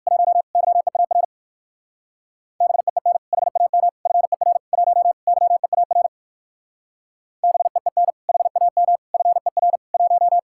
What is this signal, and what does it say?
Morse code, heard as a keyed tone, about 60 wpm, 700 Hz. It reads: 1 YEAR BEEN HAM FER 1 YEAR BEEN HAM FER 1